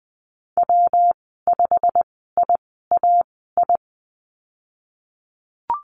Morse code, heard as a keyed tone, about 20 wpm, spaced as for 15 wpm, 700 Hz. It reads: W5IAI
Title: 167 Lesson 27g - Number 7 - Call Signs 20wpm